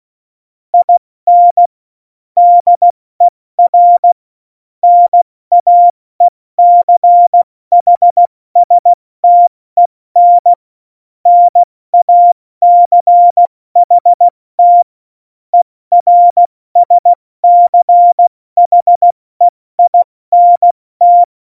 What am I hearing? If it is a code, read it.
Morse code, heard as a keyed tone, about 16 wpm, spaced as for 14 wpm, 700 Hz. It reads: IN DER NAECHSTEN NACHT ERSCHEINT
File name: Morsekurs / Witze / Witz 28. November 2017, 70 F80